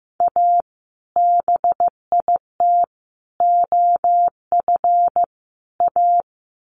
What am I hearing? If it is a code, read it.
Morse code, heard as a keyed tone, about 15 wpm, 700 Hz. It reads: A BIT OF A
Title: Sets of 4 Words 15wpm